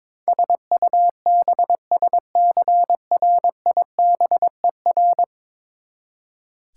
Morse code, heard as a keyed tone, about 21 wpm, 700 Hz. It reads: SUBSCRIBER
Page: morse-code-ninja-practice-22wpm